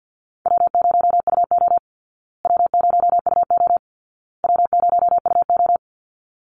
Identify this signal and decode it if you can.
Morse code, heard as a keyed tone, about 50 wpm, 700 Hz. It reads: W0UO W0UO W0UO
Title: Call Signs Repeated 2x - 50wpm